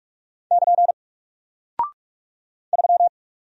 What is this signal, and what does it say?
Morse code, heard as a keyed tone, about 45 wpm, 700 Hz. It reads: <KN> 3
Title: Single Character (Letter:Number:Punctuation:Prosign) Alt 45wpm